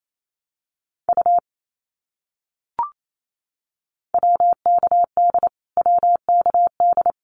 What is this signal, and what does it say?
Morse code, heard as a keyed tone, about 28 wpm, 700 Hz. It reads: U WKD WKD
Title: Abbreviations Repeated 3x - 28wpm